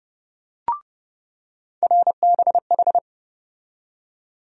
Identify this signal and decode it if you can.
Morse code, heard as a keyed tone, about 30 wpm, 700 Hz. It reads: RBH
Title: USA Call Sign Postfixes - 3 Letters 30wpm